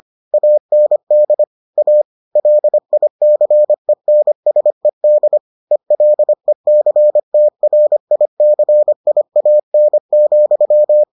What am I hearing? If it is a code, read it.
Morse code, heard as a keyed tone, about 25 wpm, 600 Hz. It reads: AND A LICENSED ELECTRICIAN,